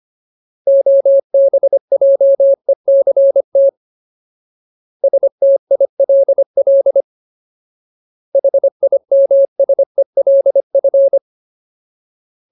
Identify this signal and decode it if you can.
Morse code, heard as a keyed tone, about 25 wpm, 550 Hz. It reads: OBJECT STILL HIMSELF
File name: Ultimate Word List / 25WPM Ultimate